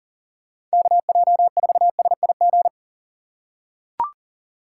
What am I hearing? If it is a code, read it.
Morse code, heard as a keyed tone, about 40 wpm, 700 Hz. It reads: KJ4SIG
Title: USA Call Signs - 6 Characters - 4x Character Spacing - 40wpm